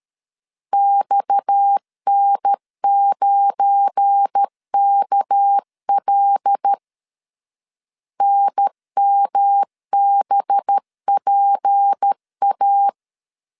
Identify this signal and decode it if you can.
Morse code, heard as a keyed tone, about 13 wpm, 800 Hz. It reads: XN9KL NMBPA
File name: LEZIONI / 7 GF9Z / 7 GF9 12wpm 2319